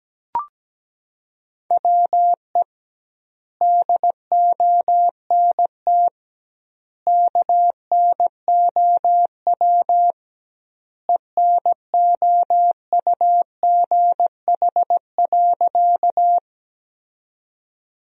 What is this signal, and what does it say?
Morse code, heard as a keyed tone, about 17 wpm, 700 Hz. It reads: WE DONT KNOW ENOUGH.
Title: 2x Word-Spacing - Sentences from Top 400 Words - 17wpm